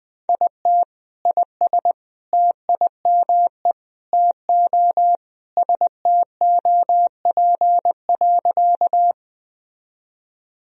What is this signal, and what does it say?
Morse code, heard as a keyed tone, about 20 wpm, 700 Hz. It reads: IT IS TIME TO STOP.
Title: Sentences from Top 1000 Words - Review Part 1 - 20wpm